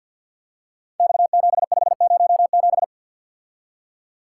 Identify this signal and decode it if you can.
Morse code, heard as a keyed tone, about 46 wpm, 700 Hz. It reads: X7507
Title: Code Groups - 5 Letter:Numbers - 50wpm